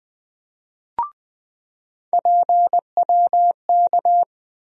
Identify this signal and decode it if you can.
Morse code, heard as a keyed tone, about 20 wpm, 700 Hz. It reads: PWK
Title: USA Call Sign Postfixes - 3 Letters 20wpm